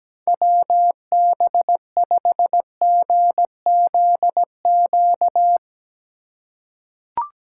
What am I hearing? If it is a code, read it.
Morse code, heard as a keyed tone, about 17 wpm, 700 Hz. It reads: WB5GZQ
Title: USA Call Signs - 6 Characters - 2x Character Spacing - 17wpm